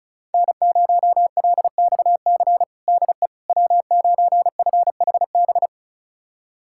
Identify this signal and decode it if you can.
Morse code, heard as a keyed tone, about 36 wpm, 700 Hz. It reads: N0LXC DE W9FHB
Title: USA Callsign DE Callsign - 5 Characters - 35wpm